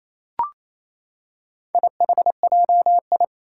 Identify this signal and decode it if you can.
Morse code, heard as a keyed tone, about 28 wpm, 700 Hz. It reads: IHJI
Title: Four Challenging Letter-Number Combinations 28wpm